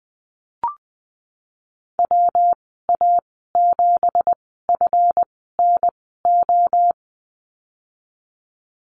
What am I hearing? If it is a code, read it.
Morse code, heard as a keyed tone, about 19 wpm, spaced as for 14 wpm, 700 Hz. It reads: WA7FNO
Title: 186 Lesson 30g - Letter b - Call Signs 20wpm